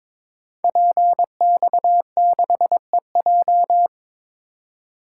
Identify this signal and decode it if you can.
Morse code, heard as a keyed tone, about 22 wpm, 700 Hz. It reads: PX6EJ
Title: Code Groups - 5 Letter:Numbers - 22wpm